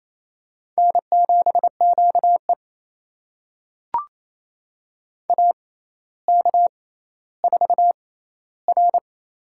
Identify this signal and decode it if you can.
Morse code, heard as a keyed tone, about 28 wpm, spaced as for 14 wpm, 700 Hz. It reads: N7QE AK4R
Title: USA Call Signs - 4 Characters - 6x Character Spacing - 28wpm